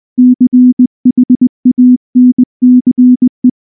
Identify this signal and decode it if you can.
Morse code, heard as a keyed tone, about 19 wpm, 250 Hz. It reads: CHANCE
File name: morse / morse-training-20WPM-1